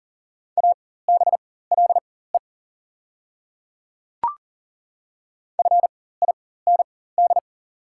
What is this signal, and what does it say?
Morse code, heard as a keyed tone, about 40 wpm, spaced as for 19 wpm, 700 Hz. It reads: ABLE FIND
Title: 4x Character Spacing - Top 500 Words - 40wpm